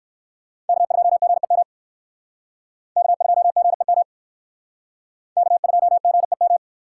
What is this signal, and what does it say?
Morse code, heard as a keyed tone, about 60 wpm, 700 Hz. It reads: K2GEM K2GEM K2GEM